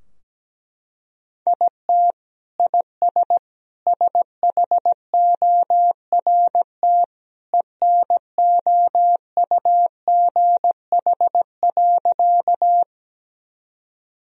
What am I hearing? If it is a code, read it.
Morse code, heard as a keyed tone, about 17 wpm, 700 Hz. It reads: IT IS SHORT ENOUGH.
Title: Sentences from Top 1000 Words - Review Part 2 - 17wpm